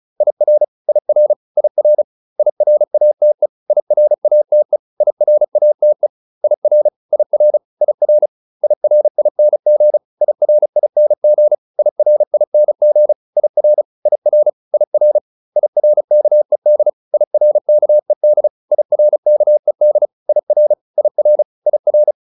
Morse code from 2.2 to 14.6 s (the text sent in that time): IRATE IRATE IRATE IR IR IR IRING IRING IRING IR IR